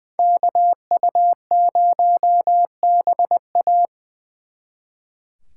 Morse code, heard as a keyed tone, about 24 wpm, 700 Hz. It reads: KU0BA
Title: Speed Racing - USA 2x2 Call Signs 20wpm